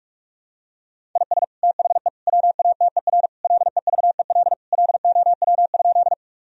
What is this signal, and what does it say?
Morse code, heard as a keyed tone, about 41 wpm, 700 Hz. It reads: IS THE WATER LEVEL LOW?
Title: Sentences from Top 1000 Words - Review Part 4 - 45wpm